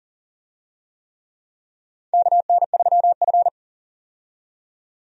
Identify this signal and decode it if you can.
Morse code, heard as a keyed tone, about 37 wpm, 690 Hz. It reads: KN3F
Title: USA Call Signs - 4 Characters - 40wpm